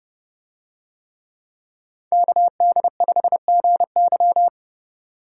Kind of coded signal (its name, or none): Morse code